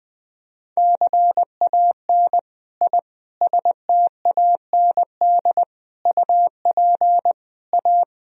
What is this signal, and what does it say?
Morse code, heard as a keyed tone, about 20 wpm, 700 Hz. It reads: CAN I STAND UP A
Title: Sentences from Top 1000 Words - Review Part 1 - 20wpm